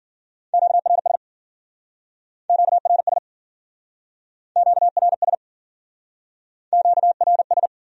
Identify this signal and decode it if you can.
Morse code, heard as a keyed tone, about 51 wpm, 700 Hz. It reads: QRS QRS QRS QRS